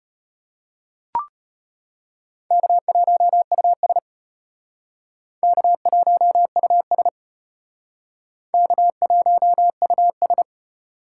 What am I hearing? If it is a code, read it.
Morse code, heard as a keyed tone, about 32 wpm, 700 Hz. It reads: K1US K1US K1US